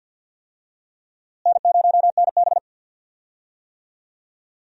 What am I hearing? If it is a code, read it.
Morse code, heard as a keyed tone, about 45 wpm, 700 Hz. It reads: N0NB
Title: USA Call Signs - 4 Characters - Repeated 2x - 50wpm